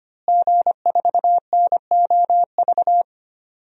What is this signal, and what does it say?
Morse code, heard as a keyed tone, about 25 wpm, 700 Hz. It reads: G4NOV
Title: DXCC-100 Call Signs - 5 Characters - Repeated 2x - 25wpm